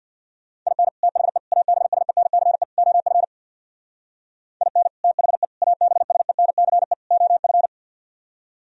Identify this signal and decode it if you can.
Morse code, heard as a keyed tone, about 60 wpm, 700 Hz. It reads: IN THE ABSENCE OF IN THE ABSENCE OF
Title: Speed-Racing - Sets of 4 Words 50wpm